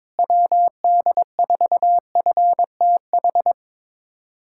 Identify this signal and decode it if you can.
Morse code, heard as a keyed tone, about 21 wpm, 700 Hz. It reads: WD4FTH
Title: USA Callsign DE Callsign - 6 Characters - 22wpm